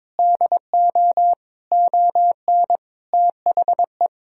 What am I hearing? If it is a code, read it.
Morse code, heard as a keyed tone, about 22 wpm, 700 Hz. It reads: DO ON THE
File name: Sentences with 9-Word Phrases - 22wpm